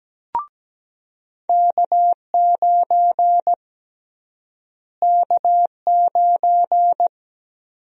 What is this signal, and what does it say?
Morse code, heard as a keyed tone, about 17 wpm, 700 Hz. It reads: K9 K9
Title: USA Call Sign Prefixes - Repeated 2x - 17wpm